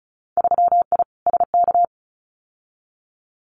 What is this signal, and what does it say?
Morse code, heard as a keyed tone, about 36 wpm, 700 Hz. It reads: 3I SK
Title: Winter Field Day Exchanges Encore 35wpm